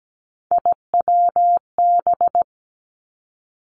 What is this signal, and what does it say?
Morse code, heard as a keyed tone, about 17 wpm, 700 Hz. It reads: IWB
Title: Three Challenging Letter-Number Combinations 17wpm